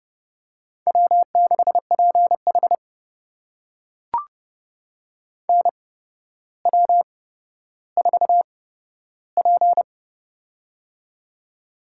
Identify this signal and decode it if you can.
Morse code, heard as a keyed tone, about 28 wpm, spaced as for 13 wpm, 700 Hz. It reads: W6PH NW4P
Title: USA Call Signs - 4 Characters - 8x Character Spacing - 30wpm